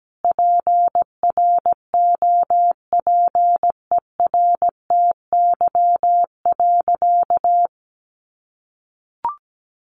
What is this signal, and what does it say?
Morse code, heard as a keyed tone, about 17 wpm, 700 Hz. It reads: PROPERTY.
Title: Sentences from Top 1000 Words 17wpm